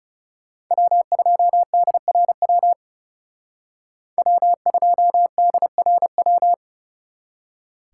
Morse code, heard as a keyed tone, about 35 wpm, 700 Hz. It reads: W2DRW W2DRW
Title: Speed Racing - USA 1x3 Call Signs 30wpm